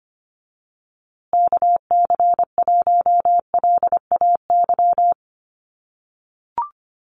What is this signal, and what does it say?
Morse code, heard as a keyed tone, about 25 wpm, 700 Hz. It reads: KC1LAY
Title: USA Call Signs Repeated 2x 25wpm